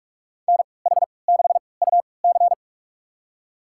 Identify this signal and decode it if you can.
Morse code, heard as a keyed tone, about 46 wpm, spaced as for 26 wpm, 700 Hz. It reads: NH6UC
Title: USA Call Signs - 5 Characters - 3x Character Spacing - 45wpm